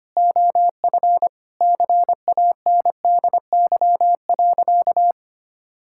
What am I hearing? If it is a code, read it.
Morse code, heard as a keyed tone, about 25 wpm, 700 Hz. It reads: OF CANDY.
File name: practice / 25wpm / Sentences with 4-Word Phrases - 25wpm